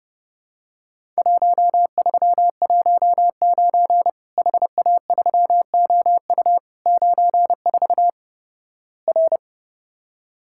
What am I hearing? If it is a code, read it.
Morse code, heard as a keyed tone, about 32 wpm, 700 Hz. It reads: 1319 HA3OU 94 R
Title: Worked All Europe DX Contest - QTC Exchange - 30wpm